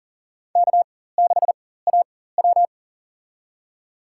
Morse code, heard as a keyed tone, about 40 wpm, spaced as for 19 wpm, 700 Hz. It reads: K6AW
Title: USA Call Signs - 4x Character Spacing - 40wpm